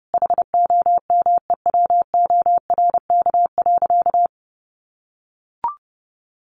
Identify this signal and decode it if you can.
Morse code, heard as a keyed tone, about 30 wpm, 700 Hz. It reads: HOMEWORK.
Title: Additional Easy Phrases 30wpm